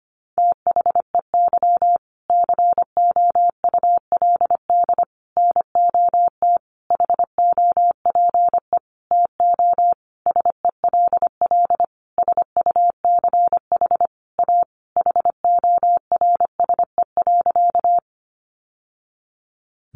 Morse code, heard as a keyed tone, about 25 wpm, 700 Hz. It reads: THEY COULD NOT HOPE TO SELL SUCH A HORSE.
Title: Sentences from Top 1100 Words 25wpm